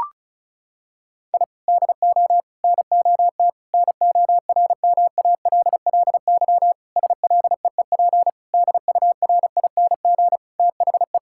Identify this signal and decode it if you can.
Morse code, heard as a keyed tone, about 36 wpm, 700 Hz. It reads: I DO NOT NORMALLY SLEEP DURING THE
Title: Sentences from Top 1000 Words - Review Part 6 - 35wpm